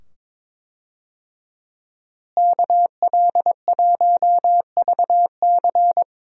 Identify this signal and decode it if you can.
Morse code, heard as a keyed tone, about 21 wpm, 700 Hz. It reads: KL1VC
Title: USA Call Signs - 5 Characters - Repeated 2x - 22wpm